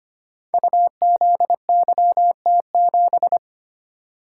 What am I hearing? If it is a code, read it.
Morse code, heard as a keyed tone, about 25 wpm, 700 Hz. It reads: UZYT7
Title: Code Groups - 5 Letter:Numbers - 25wpm